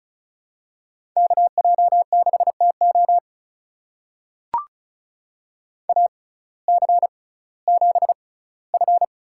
Morse code, heard as a keyed tone, about 34 wpm, spaced as for 19 wpm, 700 Hz. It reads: KJ6TO AC7F
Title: USA Call Signs - 5 Characters - 6x Character Spacing - 35wpm